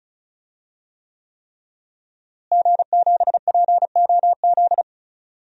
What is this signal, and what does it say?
Morse code, heard as a keyed tone, about 36 wpm, 700 Hz. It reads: G7POZ
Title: DXCC-100 Call Signs - 5 Characters - Repeated 3x - 35wpm